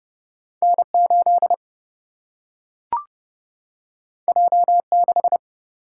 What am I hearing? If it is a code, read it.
Morse code, heard as a keyed tone, about 29 wpm, 700 Hz. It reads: N 8 J 6
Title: Single-Letter Single-Number 30wpm